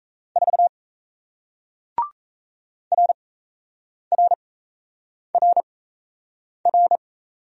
Morse code, heard as a keyed tone, about 38 wpm, spaced as for 8 wpm, 700 Hz. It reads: 4 RRRR